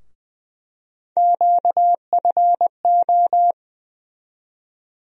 Morse code, heard as a keyed tone, about 21 wpm, 700 Hz. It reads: QFO
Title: USA Call Sign Postfixes - 3 Letters 20wpm